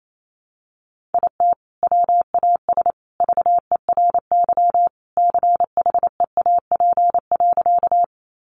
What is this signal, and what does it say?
Morse code, heard as a keyed tone, about 28 wpm, 700 Hz. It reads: IT WAS VERY CHEAP.
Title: Sentences from Top 1200 Words 28wpm